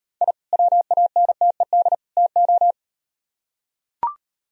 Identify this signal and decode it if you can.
Morse code, heard as a keyed tone, about 38 wpm, 700 Hz. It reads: I WANTED TO